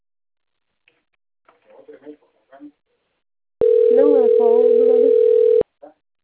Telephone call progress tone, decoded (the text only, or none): ringback tone